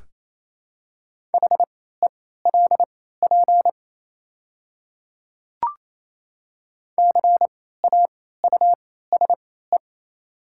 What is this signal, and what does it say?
Morse code, heard as a keyed tone, about 27 wpm, spaced as for 16 wpm, 690 Hz. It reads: HELP CAUSE
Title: 3x Character Spacing - Top 200 Words - 28wpm